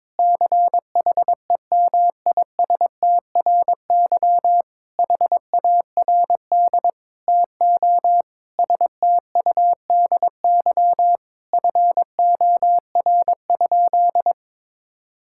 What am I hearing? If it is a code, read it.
Morse code, heard as a keyed tone, about 22 wpm, 700 Hz. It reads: CHEMISTRY HARD TO STUDY FOR?
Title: Sentences from Top 1000 Words - Review Part 6 - 22wpm